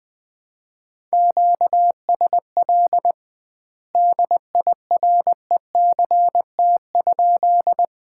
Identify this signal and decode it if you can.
Morse code, heard as a keyed tone, about 20 wpm, 700 Hz. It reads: QSL DIRECT?
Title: Common QSO Phrases - 2x Word-Spacing - 20wpm